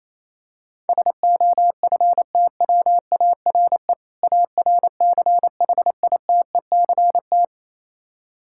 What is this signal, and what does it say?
Morse code, heard as a keyed tone, about 28 wpm, 700 Hz. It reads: SOFTWARE ARCHITECT
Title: Practice QSO - Exchange 3 - More Info - 28wpm